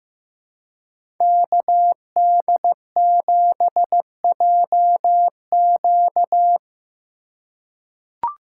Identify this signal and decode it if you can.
Morse code, heard as a keyed tone, about 15 wpm, 700 Hz. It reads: KD7JQ
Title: USA Call Signs - 5 Characters - 4x Character Spacing - 15wpm